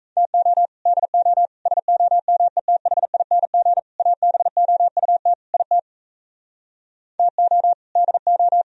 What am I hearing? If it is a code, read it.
Morse code, heard as a keyed tone, about 45 wpm, 700 Hz. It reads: TO DO SOMETHING ABOUT IT TO DO